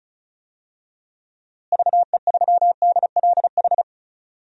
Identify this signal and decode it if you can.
Morse code, heard as a keyed tone, about 34 wpm, 700 Hz. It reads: VE3DLH
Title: DXCC-100 Call Signs - 6 Characters - Repeated 2x - 35wpm